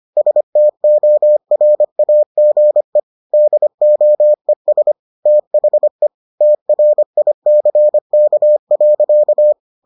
Morse code, heard as a keyed tone, about 25 wpm, 600 Hz. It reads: STORAGE DOES THE TRICK.